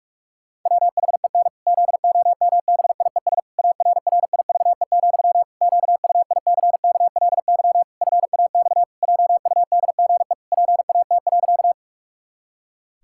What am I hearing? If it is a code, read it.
Morse code, heard as a keyed tone, about 46 wpm, 700 Hz. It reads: WHEN ZOMBIES ARRIVE, QUICKLY FAX JUDGE PAT.
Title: Pangram Challenge v2 - 45wpm